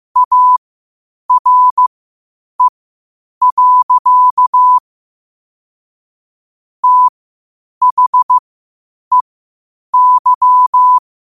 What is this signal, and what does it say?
Morse code, heard as a keyed tone, about 15 wpm, spaced as for 8 wpm, 1000 Hz. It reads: ARE. THEY
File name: qotdfiles / 8wpm